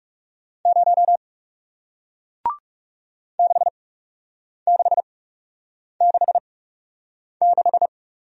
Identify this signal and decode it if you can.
Morse code, heard as a keyed tone, about 41 wpm, spaced as for 8 wpm, 700 Hz. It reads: 0 6666